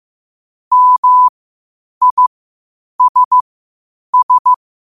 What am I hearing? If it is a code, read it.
Morse code, heard as a keyed tone, about 15 wpm, spaced as for 8 wpm, 1000 Hz. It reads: MISS
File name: qotdfiles / 8wpm